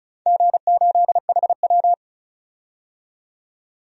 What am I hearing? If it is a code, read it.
Morse code, heard as a keyed tone, about 34 wpm, 700 Hz. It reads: G8HW
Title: DXCC-100 Call Signs - 4 Characters - 35wpm